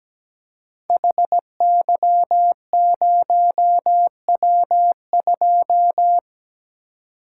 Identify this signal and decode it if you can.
Morse code, heard as a keyed tone, about 17 wpm, 700 Hz. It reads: HY0W2
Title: Code Groups - 5 Letter:Numbers - 17wpm